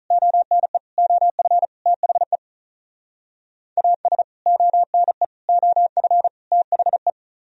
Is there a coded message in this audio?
Yes: Morse code from 0.1 to 7.1 s, heard as a keyed tone, about 46 wpm, 700 Hz. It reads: ONE OF THE AS ONE OF THE